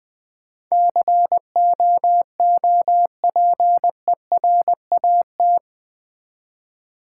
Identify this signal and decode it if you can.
Morse code, heard as a keyed tone, about 20 wpm, 700 Hz. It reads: COOPERAT
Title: Build A Word - 4 Letters At A Time - 11-Letter Words - 20wpm